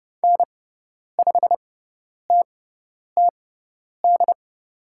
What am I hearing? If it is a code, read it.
Morse code, heard as a keyed tone, about 30 wpm, spaced as for 10 wpm, 700 Hz. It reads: N5TTD
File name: Lesson 16g - Letter c - Call Signs 30wpm